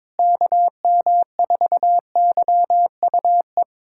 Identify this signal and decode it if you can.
Morse code, heard as a keyed tone, about 22 wpm, 700 Hz. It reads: KM4YUE
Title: USA Callsign DE Callsign - 6 Characters - 22wpm